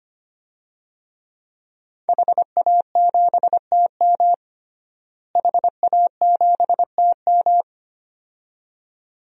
Morse code, heard as a keyed tone, about 25 wpm, 700 Hz. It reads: HA7TM HA7TM
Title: DXCC-100 Call Signs - 5 Characters - Repeated 2x - 25wpm